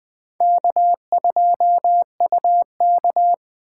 Morse code, heard as a keyed tone, about 24 wpm, 700 Hz. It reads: K2UK
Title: Speed Racing - USA 1x2 Call Signs 20wpm